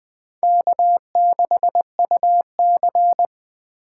Morse code, heard as a keyed tone, about 20 wpm, 700 Hz. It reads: K6UC
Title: USA Call Signs - 4 Characters - 4x Character Spacing - 20wpm